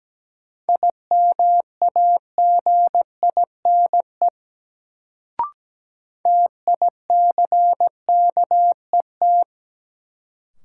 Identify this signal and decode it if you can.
Morse code, heard as a keyed tone, about 17 wpm, 700 Hz. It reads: IMAGINE TICKET
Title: Sentences from Top 1400 Words - New Word Drill - 17wpm